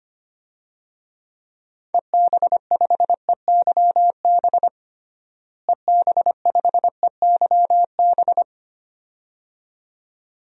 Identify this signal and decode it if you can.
Morse code, heard as a keyed tone, about 25 wpm, 700 Hz. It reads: EB5EYB EB5EYB
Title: DXCC-100 Call Signs - 6 Characters - Repeated 2x - 25wpm